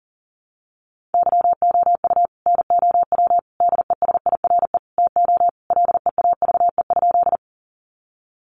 Morse code, heard as a keyed tone, about 40 wpm, 700 Hz. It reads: YOU NOW DESIRE TO LEAVE?
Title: Sentences from Top 1100 Words 40wpm